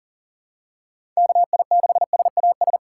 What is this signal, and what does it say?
Morse code, heard as a keyed tone, about 40 wpm, 700 Hz. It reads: KI6SAS
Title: USA Call Signs - 6 Characters - 6x Character Spacing - 40wpm